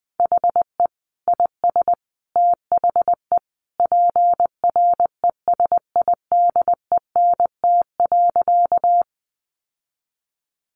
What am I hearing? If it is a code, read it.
Morse code, heard as a keyed tone, about 20 wpm, 700 Hz. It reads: HE IS THE PRESIDENT.
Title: Sentences from Top 1000 Words 20wpm